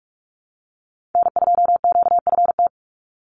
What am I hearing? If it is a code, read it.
Morse code, heard as a keyed tone, about 44 wpm, 700 Hz. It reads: N2QFT
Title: USA Call Signs 45wpm